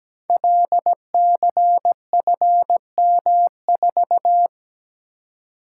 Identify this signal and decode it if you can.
Morse code, heard as a keyed tone, about 17 wpm, 700 Hz. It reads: LCFM4
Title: Code Groups - 5 Letter:Numbers - 17wpm